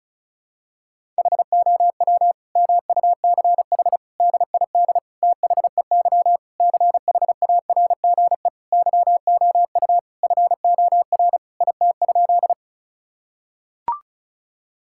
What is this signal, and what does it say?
Morse code, heard as a keyed tone, about 36 wpm, 700 Hz. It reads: HOW MUCH DID THEY CHARGE YOU FOR IT?
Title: Sentences from Top 1000 Words - Review Part 6 - 35wpm